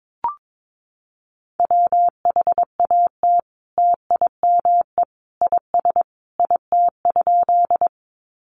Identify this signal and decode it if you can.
Morse code, heard as a keyed tone, about 22 wpm, 700 Hz. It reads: WHAT TIME IS IT?
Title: Sentences from Top 100 Words 22wpm